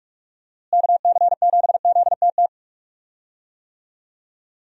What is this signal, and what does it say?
Morse code, heard as a keyed tone, about 45 wpm, 700 Hz. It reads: KC7ZTT
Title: USA Call Signs - 6 Characters - Repeated 3x - 45wpm